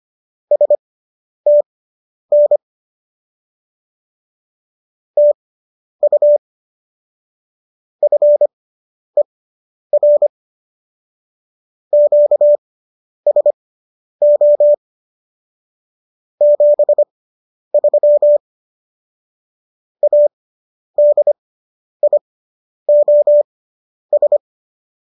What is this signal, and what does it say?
Morse code, heard as a keyed tone, about 25 wpm, spaced as for 10 wpm, 590 Hz. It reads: STN TU FER QSO 73 ADIOS